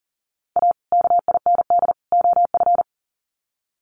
Morse code, heard as a keyed tone, about 40 wpm, 700 Hz. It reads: A KIND OF
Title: Sets of 3 Words 40wpm